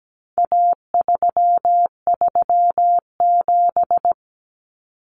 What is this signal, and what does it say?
Morse code, heard as a keyed tone, about 17 wpm, 700 Hz. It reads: A337
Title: Common Cut Numbers - 4-Digit 17wpm